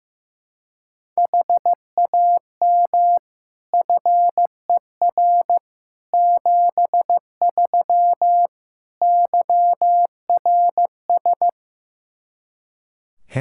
Morse code, heard as a keyed tone, about 15 wpm, 700 Hz. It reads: HAM FER 73 YRS
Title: Practice QSO - Exchange 3 - More Info - 15wpm